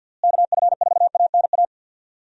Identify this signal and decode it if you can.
Morse code, heard as a keyed tone, about 47 wpm, 700 Hz. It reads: KF4ANA
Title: USA Call Signs - 6 Characters - Repeated 3x - 50wpm